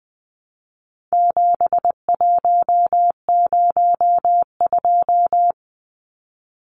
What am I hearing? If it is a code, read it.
Morse code, heard as a keyed tone, about 20 wpm, 700 Hz. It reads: 7102
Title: Numbers - 4-Digit 20wpm